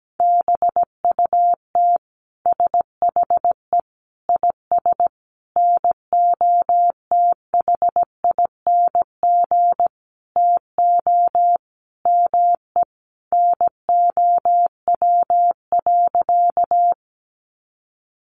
Morse code, heard as a keyed tone, about 17 wpm, 700 Hz. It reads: BUT SHE IS NOTHING TO ME NOW.
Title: Sentences from Top 500 Words 17wpm